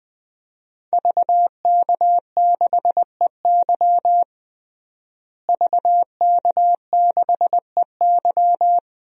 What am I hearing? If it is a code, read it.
Morse code, heard as a keyed tone, about 20 wpm, 700 Hz. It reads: VK6EY VK6EY
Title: DXCC-100 Call Signs - 5 Characters - Repeated 2x - 20wpm